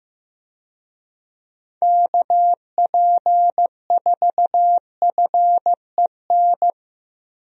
Morse code, heard as a keyed tone, about 15 wpm, 700 Hz. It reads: KP4FEN
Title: DXCC-100 Call Signs - 6 Characters - Repeated 2x - 15wpm